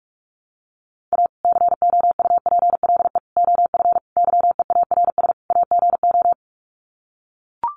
Morse code, heard as a keyed tone, about 44 wpm, 700 Hz. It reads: A COUPLE OF YEARS AGO
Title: Sets of 5 Words 45wpm